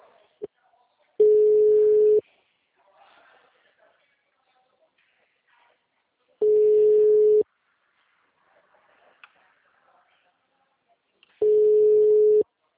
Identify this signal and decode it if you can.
Ringback tone (Europe, 425 Hz), with speech.
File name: out- -2006- - - .92462